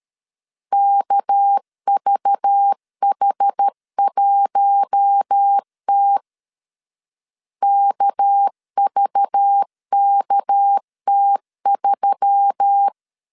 Morse code, heard as a keyed tone, about 13 wpm, 800 Hz. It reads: KVH1T KVKT3
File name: LEZIONI / 6 PKX0 / 6 PK 12wpm 2006